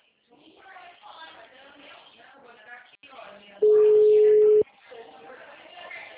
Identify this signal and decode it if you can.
Ringback tone (Europe, 425 Hz), with speech.